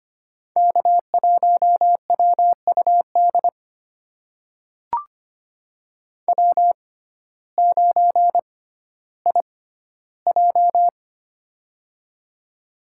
Morse code, heard as a keyed tone, about 25 wpm, spaced as for 14 wpm, 700 Hz. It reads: K1WUD W9IJ
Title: USA Call Signs - 6x Character Spacing - 25wpm